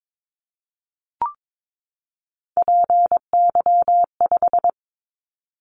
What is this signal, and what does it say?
Morse code, heard as a keyed tone, about 22 wpm, 700 Hz. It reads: PY5
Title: Callsign Prefixes for Top 50 DXCC - 22wpm